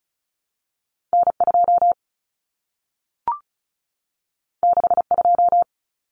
Morse code, heard as a keyed tone, about 36 wpm, 700 Hz. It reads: N2 62
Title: Common Cut Numbers - 2-Digit 35wpm